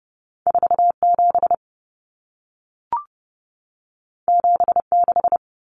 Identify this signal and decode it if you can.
Morse code, heard as a keyed tone, about 30 wpm, 700 Hz. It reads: 47 76
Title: Numbers - 2-Digit 30wpm